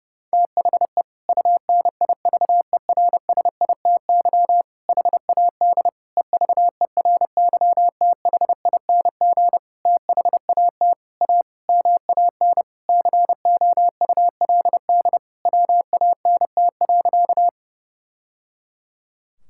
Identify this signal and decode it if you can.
Morse code, heard as a keyed tone, about 30 wpm, 700 Hz. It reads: THE UNIVERSITY HAD EVERYTHING THAT A MAN COULD WANT.
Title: Sentences from Top 1000 Words - Review Part 5 - 30wpm